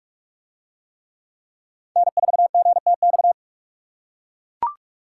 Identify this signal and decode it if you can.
Morse code, heard as a keyed tone, about 40 wpm, 700 Hz. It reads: N4GTX